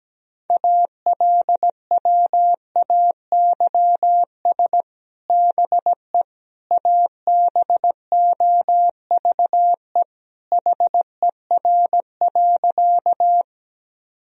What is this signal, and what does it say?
Morse code, heard as a keyed tone, about 17 wpm, 700 Hz. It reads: ALWAYS BE ABOVE HER.
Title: Sentences from Top 1000 Words - Review Part 2 - 17wpm